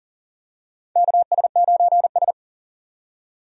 Morse code, heard as a keyed tone, about 40 wpm, 700 Hz. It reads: KS9S
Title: USA Call Signs - 4 Characters - 40wpm